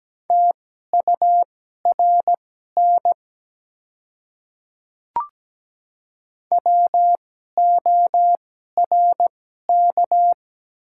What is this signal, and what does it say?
Morse code, heard as a keyed tone, about 17 wpm, spaced as for 12 wpm, 700 Hz. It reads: TURN WORK
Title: 2x Character Spacing - Top 200 Words - 17wpm